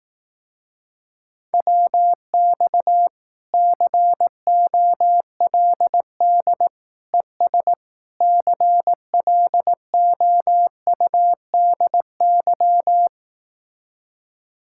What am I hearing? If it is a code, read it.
Morse code, heard as a keyed tone, about 18 wpm, 700 Hz. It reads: WX COLD ES CLOUDY
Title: Speed-Racing Common QSO Phrases Encore - 15wpm